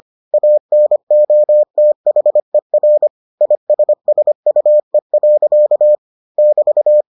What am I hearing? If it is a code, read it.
Morse code, heard as a keyed tone, about 25 wpm, 600 Hz. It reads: ANOTHER ISSUE. <BT>